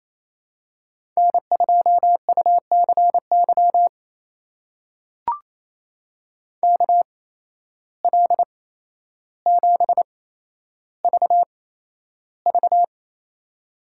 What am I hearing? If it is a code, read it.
Morse code, heard as a keyed tone, about 28 wpm, spaced as for 12 wpm, 700 Hz. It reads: N2UCY KL7VV
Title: USA Call Signs - 5 Characters - 8x Character Spacing - 28wpm